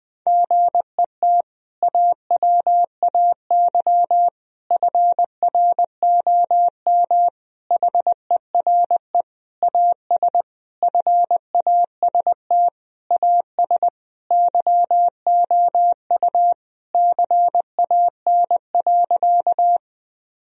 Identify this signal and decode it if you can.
Morse code, heard as a keyed tone, about 21 wpm, 700 Hz. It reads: GET AWAY FROM HERE AS FAST AS YOU CAN.
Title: Sentences from Top 1000 Words - Review Part 2 - 20wpm